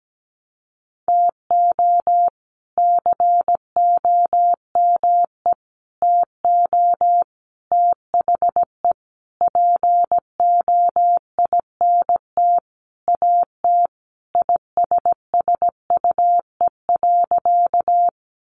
Morse code, heard as a keyed tone, about 17 wpm, 700 Hz. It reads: TO COME TO THE POINT AT ISSUE.
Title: Sentences from Top 900 Words 17wpm